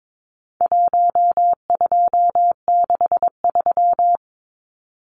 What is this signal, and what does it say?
Morse code, heard as a keyed tone, about 22 wpm, 700 Hz. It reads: 1263
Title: Numbers - 4-Digit 22wpm